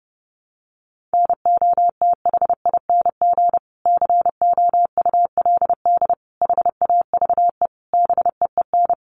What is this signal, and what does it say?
Morse code, heard as a keyed tone, about 30 wpm, 700 Hz. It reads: NOTHING COULD HAVE BEEN
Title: Sentences from Top 500 Words 30wpm